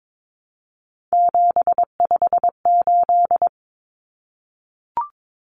Morse code, heard as a keyed tone, about 22 wpm, 700 Hz. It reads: 758
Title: Common Cut Numbers - 3-Digit 22wpm